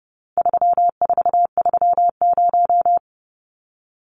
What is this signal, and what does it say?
Morse code, heard as a keyed tone, about 30 wpm, 700 Hz. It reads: 3430
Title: Numbers - 4-Digit 30wpm